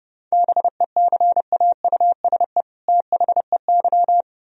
Morse code, heard as a keyed tone, about 37 wpm, 700 Hz. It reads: BECAUSE THEY
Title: Speed-Racing - Sets of 2 Words 30wpm